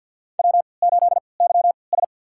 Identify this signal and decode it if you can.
Morse code, heard as a keyed tone, about 53 wpm, spaced as for 29 wpm, 700 Hz. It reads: W8YS
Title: USA Call Signs - 4 Characters - 3x Character Spacing - 50wpm